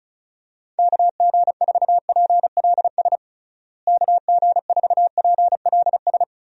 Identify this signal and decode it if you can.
Morse code, heard as a keyed tone, about 36 wpm, 700 Hz. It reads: KG4PLS KG4PLS
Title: USA Call Signs - 6 Characters - Repeated 3x - 35wpm